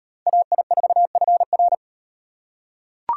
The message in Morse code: AI4FR

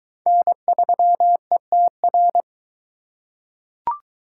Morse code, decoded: N3ETR

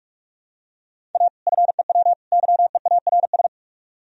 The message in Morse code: A FEW YEARS

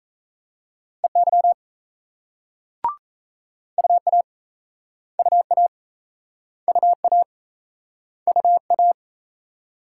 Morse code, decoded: EY UA UA UA UA

